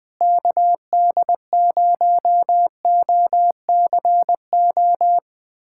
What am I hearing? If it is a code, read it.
Morse code, heard as a keyed tone, about 20 wpm, 700 Hz. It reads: KD0OCO
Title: USA Callsign DE Callsign - 6 Characters - 20wpm